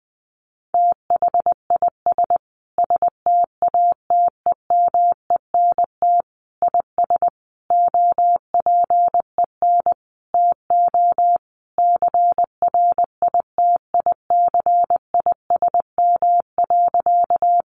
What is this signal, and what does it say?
Morse code, heard as a keyed tone, about 20 wpm, 700 Hz. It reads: THIS STATEMENT IS OPEN TO CRITICISM.
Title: Sentences from Top 1300 Words 20wpm